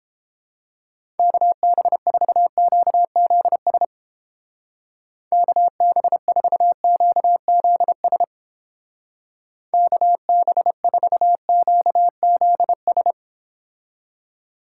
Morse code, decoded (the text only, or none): KB4QZS KB4QZS KB4QZS